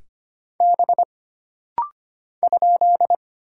Morse code, heard as a keyed tone, about 25 wpm, 700 Hz. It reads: B ?
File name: Single Character (Letter:Number:Punctuation:Prosign) - Rapid-Fire - 25wpm